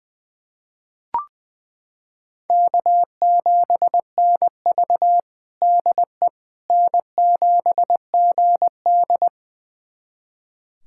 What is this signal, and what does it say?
Morse code, heard as a keyed tone, about 20 wpm, 700 Hz. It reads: K7NV DE N7GD
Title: USA Callsign DE Callsign - 4 Characters - 20wpm